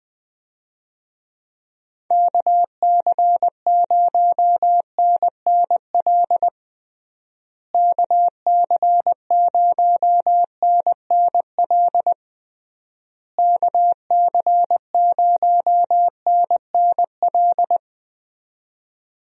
Morse code, heard as a keyed tone, about 21 wpm, 700 Hz. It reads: KC0NNL KC0NNL KC0NNL
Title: USA Call Signs - 6 Characters - Repeated 3x - 20wpm